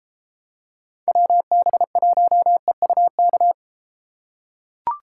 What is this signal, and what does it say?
Morse code, heard as a keyed tone, about 31 wpm, 700 Hz. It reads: WB1EUK